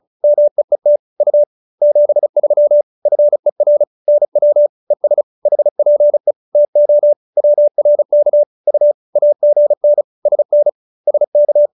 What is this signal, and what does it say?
Morse code, heard as a keyed tone, about 34 wpm, 590 Hz. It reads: MEET U 73 FER NW ES HPE TO WRK U AGN SN SK